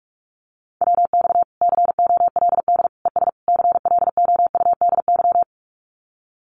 Morse code, heard as a keyed tone, about 44 wpm, 700 Hz. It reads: WX COLD ES CLOUDY
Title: Common QSO Phrases Encore 45wpm